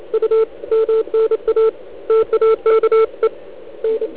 Morse code, heard as a keyed tone, about 28 wpm, 440 Hz. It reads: U MNA TAKE N